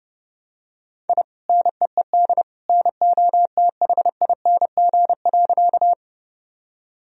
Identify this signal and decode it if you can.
Morse code, heard as a keyed tone, about 30 wpm, 700 Hz. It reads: I NEED NOTHING.
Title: Sentences from Top 1000 Words - Review Part 1 - 30wpm